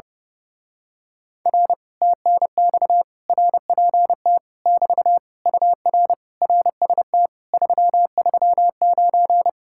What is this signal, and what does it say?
Morse code, heard as a keyed tone, about 28 wpm, 700 Hz. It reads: R TNX RPT <BT> UR RST 339
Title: Speed-Racing Common QSO Phrases - 30wpm